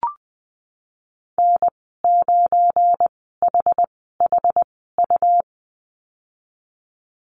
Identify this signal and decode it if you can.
Morse code, heard as a keyed tone, about 20 wpm, spaced as for 15 wpm, 700 Hz. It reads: N9HHU
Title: 193 Lesson 31g - Letter v - Call Signs 20wpm